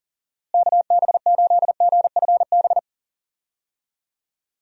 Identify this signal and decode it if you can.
Morse code, heard as a keyed tone, about 37 wpm, 700 Hz. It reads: KB8GFB
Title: USA Call Signs - 6 Characters - Repeated 2x - 40wpm